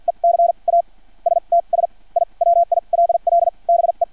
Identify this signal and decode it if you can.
Morse code, heard as a keyed tone, about 50 wpm, spaced as for 34 wpm, 680 Hz. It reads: ECASTS I WILL BE
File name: archived / CRSnet